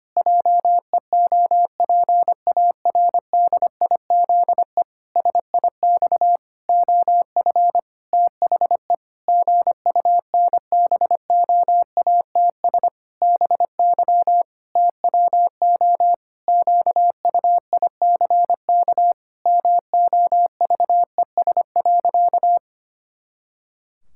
Morse code, heard as a keyed tone, about 25 wpm, 700 Hz. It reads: JEOPARDIZE SIX OF THE GUNBOATS BY TWO QUICK MOVES.
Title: Pangram Challenge v2 - 25wpm